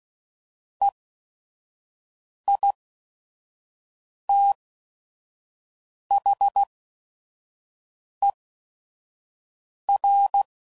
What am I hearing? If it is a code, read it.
Morse code, heard as a keyed tone, about 16 wpm, spaced as for 5 wpm, 800 Hz. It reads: EITHER
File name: Sayings / Chuckles / Squeaky